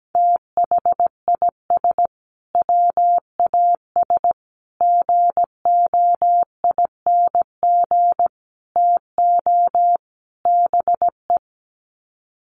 Morse code, heard as a keyed tone, about 17 wpm, 700 Hz. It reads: THIS WAS GOING TO BE
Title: Sets of 5 Words Encore2 17wpm